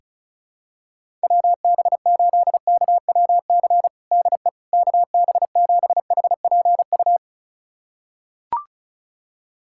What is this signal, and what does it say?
Morse code, heard as a keyed tone, about 36 wpm, 700 Hz. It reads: WB8KWC DE KB7HPU
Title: USA Callsign DE Callsign - 6 Characters - 35wpm